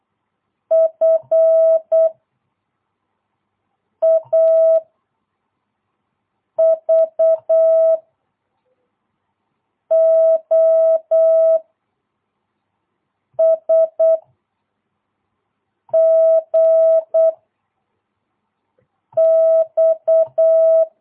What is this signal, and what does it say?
Morse code, heard as a keyed tone, about 8 wpm, 660 Hz. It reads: F A V O S G X
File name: jh8cbh / 2022cw